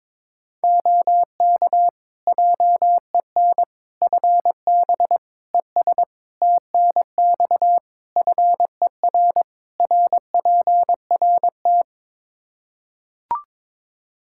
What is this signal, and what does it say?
Morse code, heard as a keyed tone, about 22 wpm, 700 Hz. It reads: OK JEN FB ES TNX FER RPRT
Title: Practice QSO - Exchange 2 - Info - 22wpm